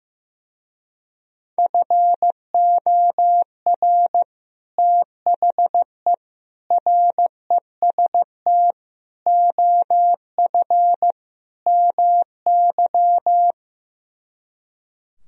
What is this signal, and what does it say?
Morse code, heard as a keyed tone, about 15 wpm, 700 Hz. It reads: FOR THE REST OF MY
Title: Speed-Racing - Sets of 5 Words 15wpm